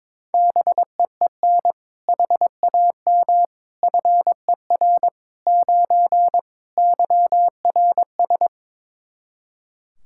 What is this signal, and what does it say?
Morse code, heard as a keyed tone, about 22 wpm, 700 Hz. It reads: BEEN HAM FER 9 YRS
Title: Practice QSO - Exchange 3 - More Info - 22wpm